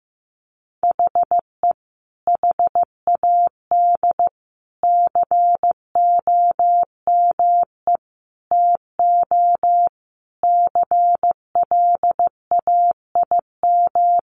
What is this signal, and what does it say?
Morse code, heard as a keyed tone, about 15 wpm, 700 Hz. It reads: HE HAD COME TO CLAIM
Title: Sentences from Top 1500 Words 15wpm